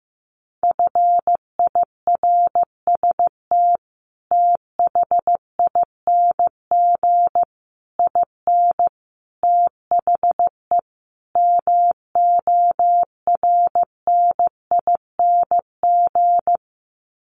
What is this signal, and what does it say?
Morse code, heard as a keyed tone, about 15 wpm, 700 Hz. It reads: FIRST THING IN THE MORNING
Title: Sets of 5 Words Encore2 15wpm